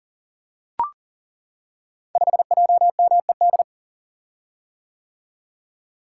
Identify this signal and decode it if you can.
Morse code, heard as a keyed tone, about 38 wpm, 700 Hz. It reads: 5JMED
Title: Code Groups - 5 Letter:Numbers - 40wpm